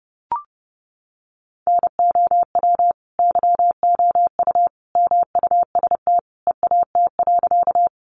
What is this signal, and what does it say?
Morse code, heard as a keyed tone, about 30 wpm, 700 Hz. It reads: NOW YOU MUST EAT.
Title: Sentences from Top 300 Words 30wpm